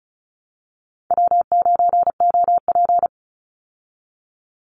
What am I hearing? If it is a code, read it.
Morse code, heard as a keyed tone, about 34 wpm, 700 Hz. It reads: W9OP
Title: Call Signs Repeated 3x - 35wpm